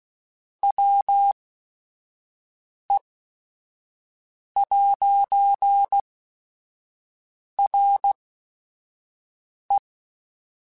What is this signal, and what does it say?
Morse code, heard as a keyed tone, about 16 wpm, spaced as for 5 wpm, 800 Hz. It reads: WE'RE
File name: Technical Subjects / Technical Subjects-mp3s / AND Gate @5